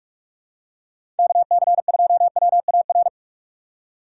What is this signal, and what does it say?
Morse code, heard as a keyed tone, about 41 wpm, 690 Hz. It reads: KC2WAR
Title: USA Call Signs - 6 Characters - 3x Character Spacing - 45wpm